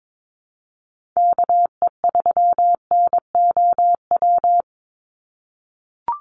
Morse code, heard as a keyed tone, about 22 wpm, 700 Hz. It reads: KE3NOW
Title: USA Call Signs 22wpm